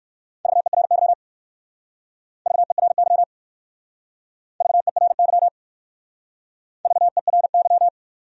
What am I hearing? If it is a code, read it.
Morse code, heard as a keyed tone, about 54 wpm, 700 Hz. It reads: VERY VERY VERY VERY
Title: Speed-Racing Top 500 Words 45wpm